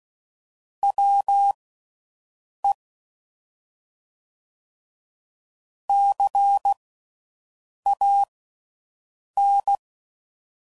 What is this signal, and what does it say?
Morse code, heard as a keyed tone, about 16 wpm, spaced as for 6 wpm, 800 Hz. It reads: WE CAN